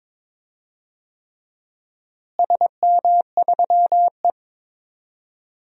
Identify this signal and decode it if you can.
Morse code, heard as a keyed tone, about 21 wpm, 700 Hz. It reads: SM3E
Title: DXCC-100 Call Signs - 4 Characters - Repeated 3x - 22wpm